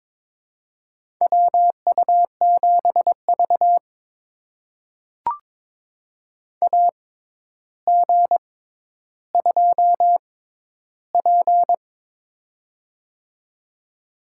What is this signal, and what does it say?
Morse code, heard as a keyed tone, about 22 wpm, spaced as for 11 wpm, 700 Hz. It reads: WU7V AG2P
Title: USA Call Signs - 4 Characters - 6x Character Spacing - 22wpm